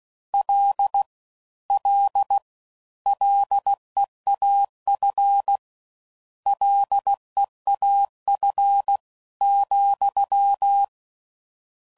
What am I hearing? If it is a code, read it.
Morse code, heard as a keyed tone, about 16 wpm, 800 Hz. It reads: L L LEAF LEAF ,